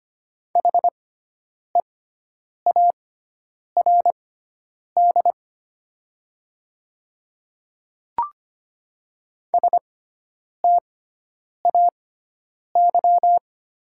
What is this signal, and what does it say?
Morse code, heard as a keyed tone, about 25 wpm, spaced as for 9 wpm, 700 Hz. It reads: HEARD STAY
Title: 6x Character Spacing - Top 500 Words - 25wpm